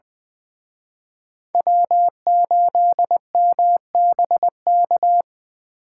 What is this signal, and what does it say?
Morse code, heard as a keyed tone, about 24 wpm, 700 Hz. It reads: W8MBK
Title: Speed Racing - USA 1x3 Call Signs 20wpm